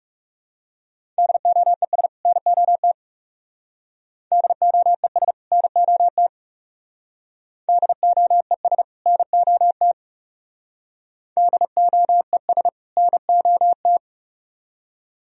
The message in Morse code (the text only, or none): DOES NOT DOES NOT DOES NOT DOES NOT